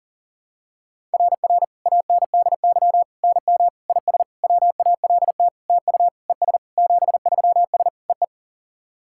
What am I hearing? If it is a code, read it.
Morse code, heard as a keyed tone, about 40 wpm, 700 Hz. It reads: RR ANDY NM IS WALT TU ES 73S EE
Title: SOTA QSO Practice 40wpm